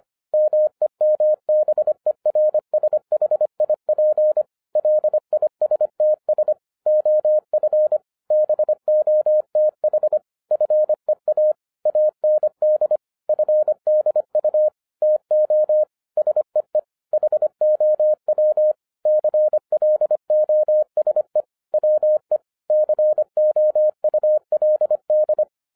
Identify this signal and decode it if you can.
Morse code, heard as a keyed tone, about 25 wpm, 610 Hz. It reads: MEMBERSHIP LISTS OF BOTH FEA AND FDU TO SEE HOW CLOSE WE COULD
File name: 2012-01-mp3 / n14-FUN WITH NUMBERS-25wpm